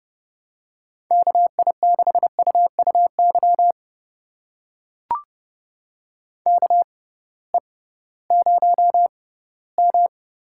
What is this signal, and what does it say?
Morse code, heard as a keyed tone, about 32 wpm, spaced as for 18 wpm, 700 Hz. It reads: KI6UUY KE0M
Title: USA Call Signs - 6 Characters - 6x Character Spacing - 30wpm